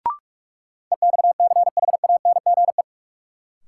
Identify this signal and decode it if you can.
Morse code, heard as a keyed tone, about 48 wpm, 700 Hz. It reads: EXCHANGE
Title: Sets of 5 Words - New Word Drill - 45wpm